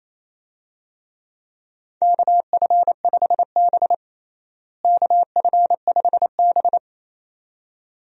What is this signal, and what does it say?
Morse code, heard as a keyed tone, about 28 wpm, 700 Hz. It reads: KF5B KF5B
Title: USA Call Signs - 4 Characters - Repeated 2x - 28wpm